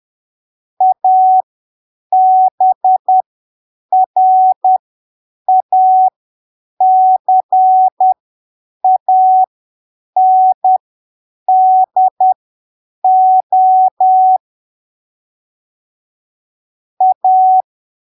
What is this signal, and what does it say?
Morse code, heard as a keyed tone, about 10 wpm, spaced as for 7 wpm, 750 Hz. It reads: ABRACANDO A
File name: arquivos / texto / texto 19 10ppm